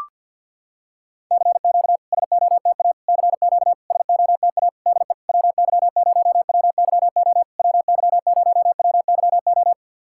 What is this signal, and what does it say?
Morse code, heard as a keyed tone, about 53 wpm, 700 Hz. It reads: CQ SOTA CQ SOTA DE WY0WYO WY0WYO
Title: SOTA QSO Practice 50wpm